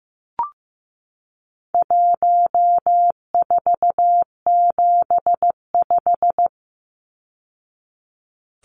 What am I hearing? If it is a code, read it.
Morse code, heard as a keyed tone, about 15 wpm, 700 Hz. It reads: 1475
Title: Numbers - 4-Digit 15wpm